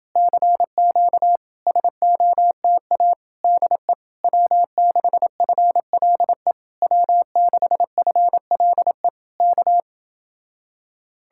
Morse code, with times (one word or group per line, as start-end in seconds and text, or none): 0.2-1.4 CQ
1.7-3.1 SOTA
3.4-3.9 DE
4.2-6.5 W6FLE
6.8-9.1 W6FLE
9.4-9.8 K